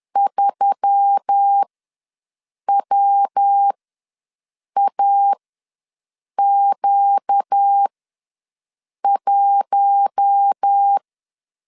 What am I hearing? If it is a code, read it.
Morse code, heard as a keyed tone, about 10 wpm, spaced as for 6 wpm, 800 Hz. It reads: 3WAQ1
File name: LEZIONI / 8 YJ45 / 8 YJ4 ws 10wpm 4020